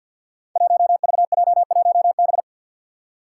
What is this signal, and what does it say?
Morse code, heard as a keyed tone, about 52 wpm, 700 Hz. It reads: 1VJ1B
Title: Five Challenging Letter-Number Combinations 50wpm